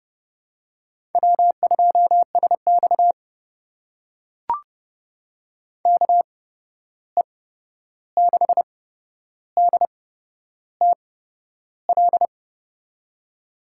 Morse code, heard as a keyed tone, about 30 wpm, spaced as for 11 wpm, 700 Hz. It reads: W2SX KE6DTL
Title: USA Call Signs - 8x Character Spacing - 30wpm